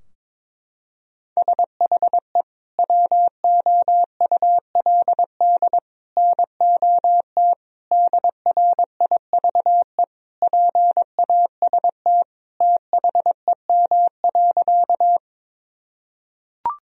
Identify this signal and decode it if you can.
Morse code, heard as a keyed tone, about 23 wpm, 700 Hz. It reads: SHE WOULD NOT DRIVE PAST THEM.
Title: Sentences from Top 1000 Words - Review Part 3 - 22wpm